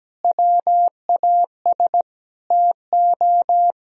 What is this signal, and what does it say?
Morse code, heard as a keyed tone, about 17 wpm, 700 Hz. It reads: WAS TO
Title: Speed-Racing - Sets of 5 Words 17wpm